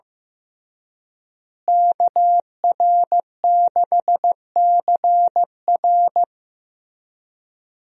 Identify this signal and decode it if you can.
Morse code, heard as a keyed tone, about 15 wpm, 700 Hz. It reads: KR6CR
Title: USA Call Signs - 5 Characters - Repeated 3x - 15wpm